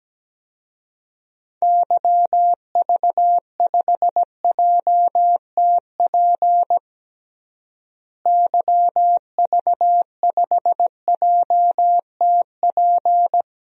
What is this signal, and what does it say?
Morse code, heard as a keyed tone, about 17 wpm, 700 Hz. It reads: YV5JTP YV5JTP
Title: DXCC-100 Call Signs - 6 Characters - Repeated 3x - 17wpm